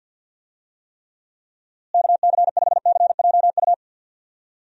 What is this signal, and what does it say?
Morse code, heard as a keyed tone, about 52 wpm, 700 Hz. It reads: KC5CJU
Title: USA Call Signs - 6 Characters - Repeated 3x - 50wpm